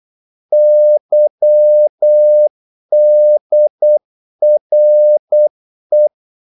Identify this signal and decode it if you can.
Morse code, heard as a keyed tone, about 8 wpm, 600 Hz. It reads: YDRE
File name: Korte ord DK 8wpm